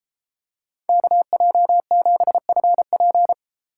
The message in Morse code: KJ7FP

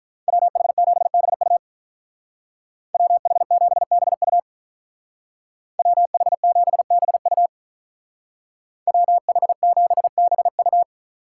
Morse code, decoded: WH7BU WH7BU WH7BU WH7BU

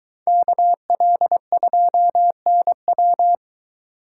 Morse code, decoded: KL2NW

